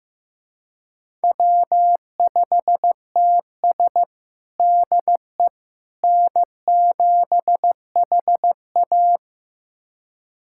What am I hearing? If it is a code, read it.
Morse code, heard as a keyed tone, about 15 wpm, 700 Hz. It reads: W5TS DE N7HA
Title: USA Callsign DE Callsign - 4 Characters - 15wpm